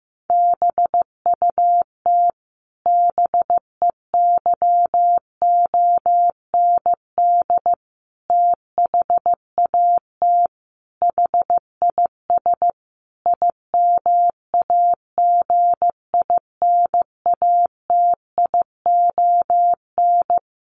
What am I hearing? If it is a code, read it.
Morse code, heard as a keyed tone, about 15 wpm, 700 Hz. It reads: BUT BEYOND THAT HIS IMAGINATION
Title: Sentences from Top 1300 Words 15wpm